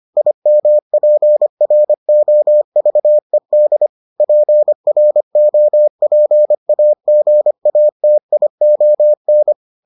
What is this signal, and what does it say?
Morse code, heard as a keyed tone, about 25 wpm, 600 Hz. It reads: IMPROVED PROPAGATION